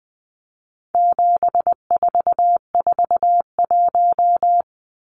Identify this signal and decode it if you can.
Morse code, heard as a keyed tone, about 20 wpm, 700 Hz. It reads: 7441
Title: Numbers - 4-Digit 20wpm